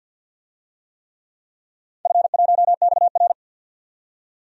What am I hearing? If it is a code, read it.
Morse code, heard as a keyed tone, about 51 wpm, 700 Hz. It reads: F1XR
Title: DXCC-100 Call Signs - 4 Characters - 50wpm